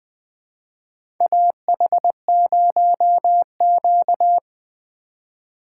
Morse code, heard as a keyed tone, about 20 wpm, 700 Hz. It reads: AH0Q
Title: USA Call Signs - 4 Characters - 6x Character Spacing - 20wpm